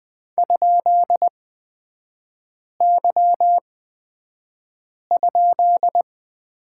Morse code, heard as a keyed tone, about 20 wpm, spaced as for 5 wpm, 700 Hz. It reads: ?Y?
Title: 224 Lesson 36e - BT - Three Characters 20wpm